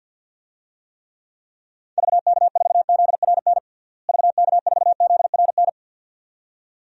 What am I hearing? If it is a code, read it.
Morse code, heard as a keyed tone, about 47 wpm, 700 Hz. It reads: VK4ZRN VK4ZRN
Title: DXCC-100 Call Signs - 6 Characters - Repeated 2x - 50wpm